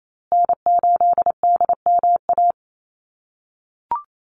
Morse code, heard as a keyed tone, about 28 wpm, 700 Hz. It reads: N8DMA
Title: USA Call Signs Repeated 3x 28wpm